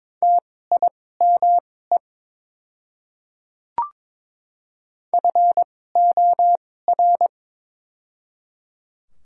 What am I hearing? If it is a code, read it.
Morse code, heard as a keyed tone, about 23 wpm, spaced as for 16 wpm, 700 Hz. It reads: TIME FOR
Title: 2x Character Spacing - Top 100 Words - 22wpm